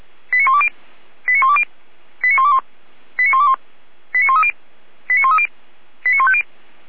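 Five-tone selective call (ZVEI), heard as quick runs of five tones, tones of about 70 ms. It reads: 89120 89120 89121 89121 89130 89130 89150
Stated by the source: ZVEI five-tone selective call received on SDR.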